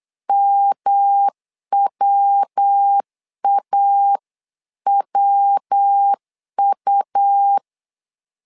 Morse code, heard as a keyed tone, about 8 wpm, 800 Hz. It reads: MWAWU
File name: LEZIONI / 3 MWRO / 3 MW 8wpm 2501